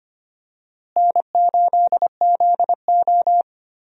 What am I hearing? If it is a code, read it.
Morse code, heard as a keyed tone, about 25 wpm, 700 Hz. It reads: N8ZO
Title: USA Call Signs - 4 Characters - Repeated 2x - 25wpm